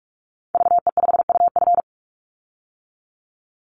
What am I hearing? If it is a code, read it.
Morse code, heard as a keyed tone, about 46 wpm, 700 Hz. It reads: VE5UF
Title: Call Signs - 45wpm